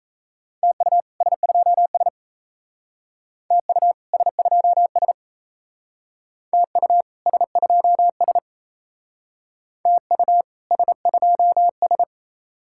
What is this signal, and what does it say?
Morse code, heard as a keyed tone, about 36 wpm, 700 Hz. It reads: TU S2S TU S2S TU S2S TU S2S